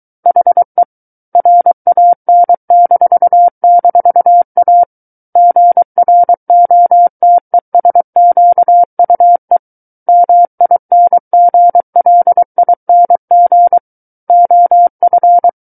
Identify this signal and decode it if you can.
Morse code, heard as a keyed tone, about 23 wpm, 700 Hz. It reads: HE RAN--A GROTESQUE MINGLING OF
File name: Book One / 14-Chapter 14 - 23 WPM